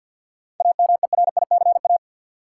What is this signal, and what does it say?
Morse code, heard as a keyed tone, about 52 wpm, 700 Hz. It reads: AMERICA
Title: Sets of 5 Words - New Word Drill - 50wpm